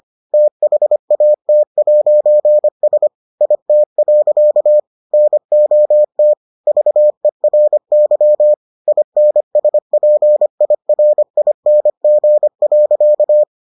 Morse code, heard as a keyed tone, about 25 wpm, 600 Hz. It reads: THAT'S IT. NOT VERY INSPIRING.